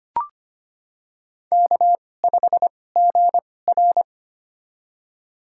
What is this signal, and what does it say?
Morse code, heard as a keyed tone, about 25 wpm, spaced as for 18 wpm, 700 Hz. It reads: K5GR
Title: USA Call Signs - 4 Characters - 2x Character Spacing - 25wpm